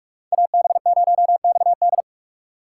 Morse code, heard as a keyed tone, about 46 wpm, 700 Hz. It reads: AB0XD
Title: USA Call Signs - 5 Characters - 6x Character Spacing - 45wpm